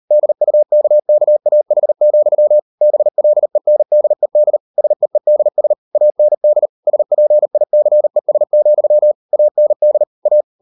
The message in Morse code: DUKKAH, BLENDED SEEDS AND SPICES, AND A

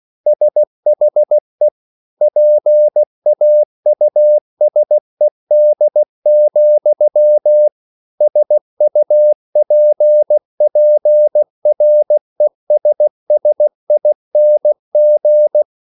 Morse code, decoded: SHE PAUSED, SUPPRESSING

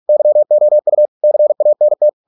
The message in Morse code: YOU CANT

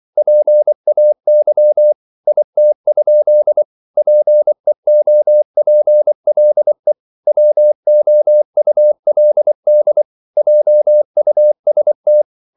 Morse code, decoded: PAY IT? PEOPLE WOULD JUST

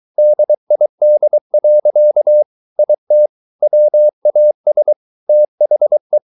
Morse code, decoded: DID. IT WAS THE

